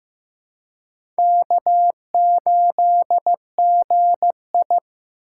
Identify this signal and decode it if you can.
Morse code, heard as a keyed tone, about 15 wpm, 700 Hz. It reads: K8GI
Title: USA Call Signs - 4 Characters - 3x Character Spacing - 15wpm